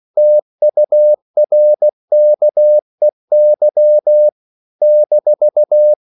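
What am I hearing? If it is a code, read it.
Morse code, heard as a keyed tone, about 16 wpm, 600 Hz. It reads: TURKEY -
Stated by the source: CW practice of national capitals at 16 words per minute, from ON6ZQ.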